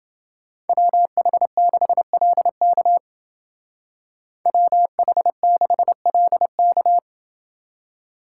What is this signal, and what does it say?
Morse code, heard as a keyed tone, about 28 wpm, 700 Hz. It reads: WH6LK WH6LK